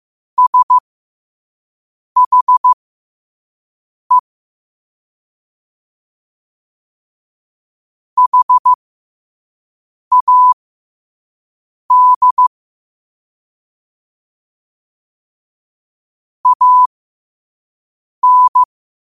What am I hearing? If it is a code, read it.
Morse code, heard as a keyed tone, about 15 wpm, spaced as for 5 wpm, 1000 Hz. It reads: SHE HAD AN
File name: qotdfiles / 5wpm